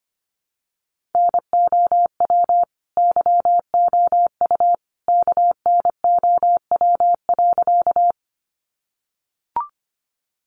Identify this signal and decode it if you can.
Morse code, heard as a keyed tone, about 25 wpm, 700 Hz. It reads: NOW YOU KNOW.
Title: Sentences from Top 100 Words 25wpm